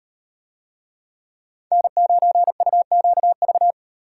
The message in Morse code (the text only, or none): N9UQV